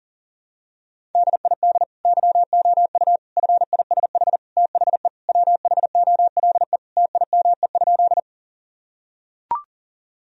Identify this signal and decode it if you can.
Morse code, heard as a keyed tone, about 40 wpm, 700 Hz. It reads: DID YOU FISH THE WHOLE TIME?
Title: Sentences from Top 1000 Words - Review Part 1 - 40wpm